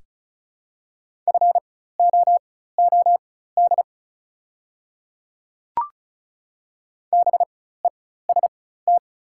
Morse code, heard as a keyed tone, about 36 wpm, spaced as for 16 wpm, 690 Hz. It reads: FOOD BEST
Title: 4x Character Spacing - Top 500 Words - 35wpm